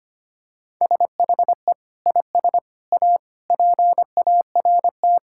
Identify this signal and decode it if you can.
Morse code, heard as a keyed tone, about 25 wpm, 700 Hz. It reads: SHE IS A PART
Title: Sentences from Top 1000 Words - Review Part 5 - 25wpm